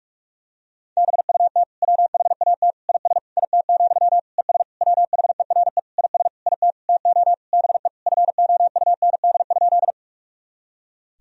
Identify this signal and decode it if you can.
Morse code, heard as a keyed tone, about 46 wpm, 700 Hz. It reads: BUT WHAT IS IT, ES WHERE IS IT TO BE FOUND?
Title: Sentences from Top 300 Words and Abbreviations 45wpm